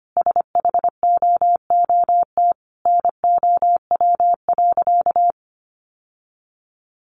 Morse code, heard as a keyed tone, about 25 wpm, 700 Hz. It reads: SHOOT NOW.
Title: Sentences from Top 1200 Words 25wpm